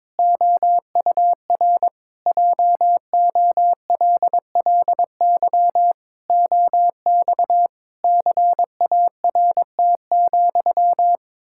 Morse code, heard as a keyed tone, about 22 wpm, 700 Hz. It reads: OUR JOLLY OX CART,
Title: Pangram Challenge v2 - 22wpm